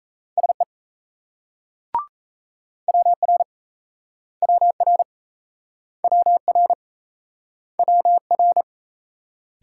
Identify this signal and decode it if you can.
Morse code, heard as a keyed tone, about 36 wpm, 700 Hz. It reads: SE WR WR WR WR